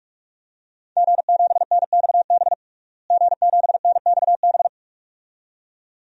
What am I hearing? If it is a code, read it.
Morse code, heard as a keyed tone, about 45 wpm, 700 Hz. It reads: G7NXB G7NXB
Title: DXCC-100 Call Signs - 5 Characters - Repeated 2x - 45wpm